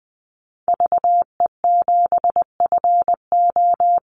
Morse code, heard as a keyed tone, about 20 wpm, 700 Hz. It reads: VE7FO
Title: Call Signs Repeated 3x - 20wpm